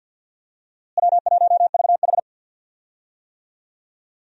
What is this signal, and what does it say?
Morse code, heard as a keyed tone, about 52 wpm, 700 Hz. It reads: W1VH
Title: USA Call Signs - 4 Characters - Repeated 3x - 50wpm